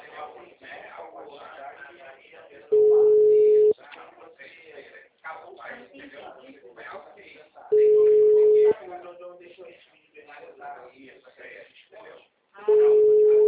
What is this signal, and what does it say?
Ringback tone (Europe, 425 Hz), with speech.
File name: out- -2002- - - .36820